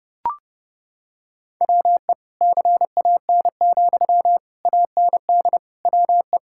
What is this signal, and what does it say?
Morse code, heard as a keyed tone, about 28 wpm, 700 Hz. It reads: WE CAN, AND WE
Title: Sentences from Top 1000 Words - Review Part 6 - 30wpm